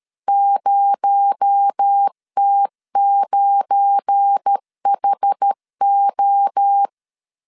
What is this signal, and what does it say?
Morse code, heard as a keyed tone, about 13 wpm, 800 Hz. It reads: 0T9HO
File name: LEZIONI / 7 GF9Z / 7 GF9 12wpm 2319